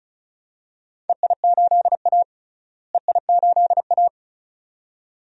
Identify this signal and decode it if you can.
Morse code, heard as a keyed tone, about 36 wpm, 690 Hz. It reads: EI8A EI8A
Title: DXCC-100 Call Signs - 4 Characters - Repeated 2x - 35wpm